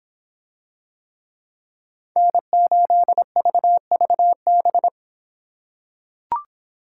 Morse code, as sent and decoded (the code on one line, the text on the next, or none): -. ---.. ...- ...- -...
N8VVB